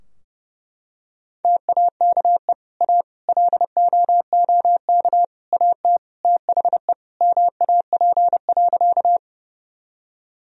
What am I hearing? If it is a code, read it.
Morse code, heard as a keyed tone, about 30 wpm, 700 Hz. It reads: TAKE A LOOK AT THE MAP.
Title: Sentences from Top 1000 Words - Review Part 2 - 30wpm